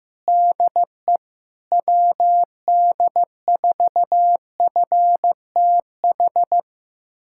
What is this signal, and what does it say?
Morse code, heard as a keyed tone, about 15 wpm, 700 Hz. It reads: DE WD4FTH
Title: USA Callsign DE Callsign - 6 Characters - 15wpm